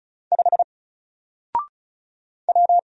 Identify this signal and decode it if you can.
Morse code, heard as a keyed tone, about 34 wpm, 700 Hz. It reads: 5 W
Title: Single Letter-Number Alt 35wpm